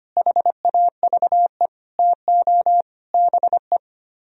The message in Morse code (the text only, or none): HAVE TO BE